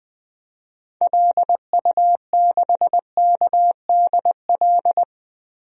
Morse code, heard as a keyed tone, about 20 wpm, 700 Hz. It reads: LU6KDL
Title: DXCC-100 Call Signs - 6 Characters - Repeated 3x - 20wpm